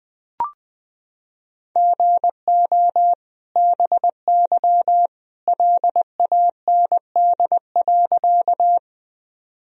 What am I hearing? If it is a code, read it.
Morse code, heard as a keyed tone, about 20 wpm, 700 Hz. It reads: GO BY LAND.
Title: Sentences from Top 1000 Words - Review Part 1 - 20wpm